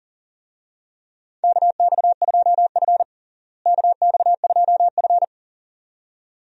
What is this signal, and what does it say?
Morse code, heard as a keyed tone, about 40 wpm, 700 Hz. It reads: KX2F KX2F
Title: USA Call Signs - 4 Characters - Repeated 2x - 40wpm